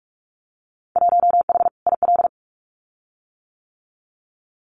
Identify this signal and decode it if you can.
Morse code, heard as a keyed tone, about 46 wpm, 700 Hz. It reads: 1H IL
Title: Winter Field Day Exchanges 45wpm